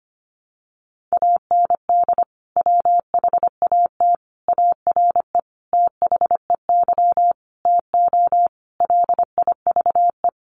Morse code, heard as a keyed tone, about 25 wpm, 700 Hz. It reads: AND WHAT ARE THEY TO LIVE
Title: Sentences from Top 1100 Words 25wpm